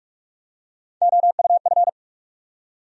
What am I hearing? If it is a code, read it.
Morse code, heard as a keyed tone, about 47 wpm, 700 Hz. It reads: OUF
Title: USA Call Sign Postfixes - 3 Letters 45wpm